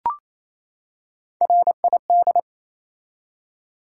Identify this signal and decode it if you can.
Morse code, heard as a keyed tone, about 28 wpm, 700 Hz. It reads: RID
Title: Three Letter Words - 28wpm